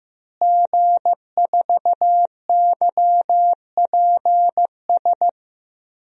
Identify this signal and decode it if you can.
Morse code, heard as a keyed tone, about 16 wpm, 700 Hz. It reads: G4YPS
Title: DXCC-100 Call Signs - 5 Characters - Repeated 2x - 15wpm